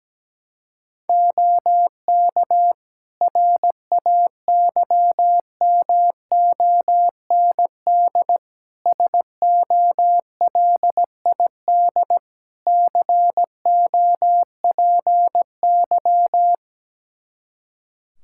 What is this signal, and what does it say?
Morse code, heard as a keyed tone, about 17 wpm, 700 Hz. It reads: OK RAYMOND SOLID COPY
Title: Practice QSO - Exchange 3 - More Info - 17wpm